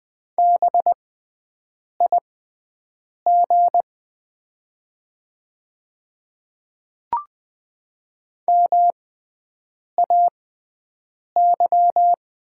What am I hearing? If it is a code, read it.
Morse code, heard as a keyed tone, about 20 wpm, spaced as for 7 wpm, 700 Hz. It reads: BIG MAY
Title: 6x Character Spacing - Top 200 Words - 20wpm